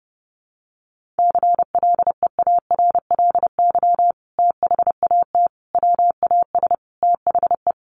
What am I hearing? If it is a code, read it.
Morse code, heard as a keyed tone, about 30 wpm, 700 Hz. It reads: CLEARLY THAT WAS THE
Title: Sentences from Top 1300 Words 30wpm